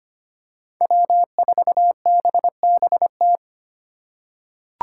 Morse code, heard as a keyed tone, about 25 wpm, 700 Hz. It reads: W4BBT
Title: K1USN SST QSO Practice - 25wpm